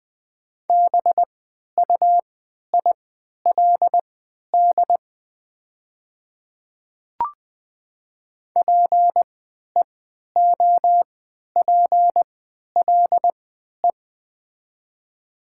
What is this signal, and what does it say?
Morse code, heard as a keyed tone, about 20 wpm, spaced as for 11 wpm, 700 Hz. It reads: BUILD PEOPLE
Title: 3x Character Spacing - Top 300 Words - 20wpm